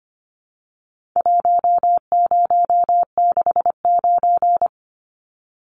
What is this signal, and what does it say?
Morse code, heard as a keyed tone, about 25 wpm, 700 Hz. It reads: 1069
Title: Numbers - Contest Style - 4-Digit 25wpm